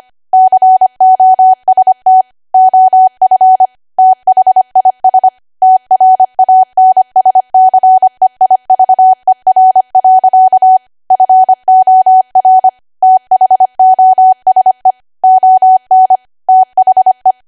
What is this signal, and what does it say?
Morse code, heard as a keyed tone, about 25 wpm, 750 Hz. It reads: COST OF THIS TRANSCEIVER. FOR THOSE ON THE
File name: Archive / 25 WPM / 25WPM